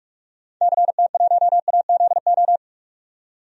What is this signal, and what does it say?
Morse code, heard as a keyed tone, about 46 wpm, 700 Hz. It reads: CT1AZO
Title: DXCC-100 Call Signs - 6 Characters - 45wpm